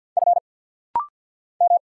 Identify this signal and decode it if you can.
Morse code, heard as a keyed tone, about 50 wpm, 700 Hz. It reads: F M
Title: Single Letters - Rapid-Fire 50wpm